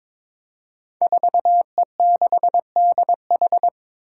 Morse code, heard as a keyed tone, about 21 wpm, 700 Hz. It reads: 4E6DH
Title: Five Challenging Letter-Number Combinations 22wpm